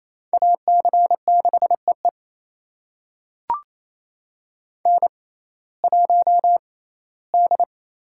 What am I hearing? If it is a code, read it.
Morse code, heard as a keyed tone, about 28 wpm, spaced as for 17 wpm, 700 Hz. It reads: AC6EE N1D
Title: USA Call Signs - 5 Characters - 6x Character Spacing - 28wpm